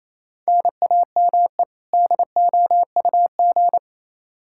Morse code, heard as a keyed tone, about 27 wpm, 700 Hz. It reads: NAME DOUG
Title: Practice QSO - Exchange 1 - Beginning - 28wpm